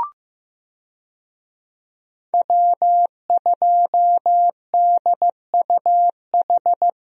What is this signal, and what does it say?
Morse code, heard as a keyed tone, about 15 wpm, 700 Hz. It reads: W2DUH
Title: USA Call Signs - 5 Characters - 15wpm